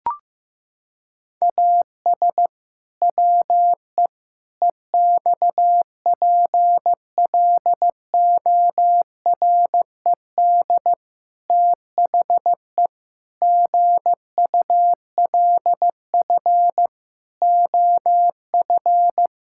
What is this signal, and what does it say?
Morse code, heard as a keyed tone, about 15 wpm, 700 Hz. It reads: AS WE EXPLORED THE GULF OF
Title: Pangram Challenge v2 - 15wpm